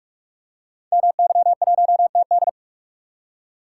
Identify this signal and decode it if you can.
Morse code, heard as a keyed tone, about 44 wpm, 700 Hz. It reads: MY1TD
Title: Code Groups - 5 Letter:Numbers - 45wpm